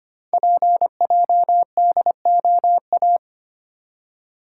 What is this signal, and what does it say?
Morse code, heard as a keyed tone, about 25 wpm, 700 Hz. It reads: PJDOA
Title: Code Groups - 5 Letters - 25wpm